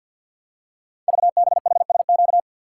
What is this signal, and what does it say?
Morse code, heard as a keyed tone, about 53 wpm, 700 Hz. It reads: VBHSQ
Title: Code Groups - 5 Letter:Numbers - 50wpm